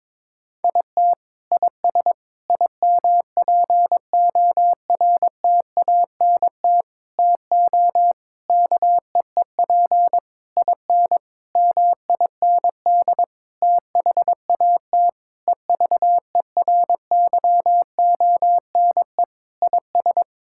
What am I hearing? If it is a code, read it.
Morse code, heard as a keyed tone, about 21 wpm, 700 Hz. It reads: IT IS IMPORTANT TO KEEP IN MIND THAT EVERYONE IS
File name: practice / 22wpm / Sentences with 8-Word Phrases - 22wpm